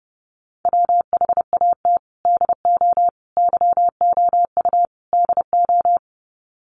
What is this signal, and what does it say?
Morse code, heard as a keyed tone, about 30 wpm, 700 Hz. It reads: WHAT DO YOU DO
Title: Sets of 4 Words 30wpm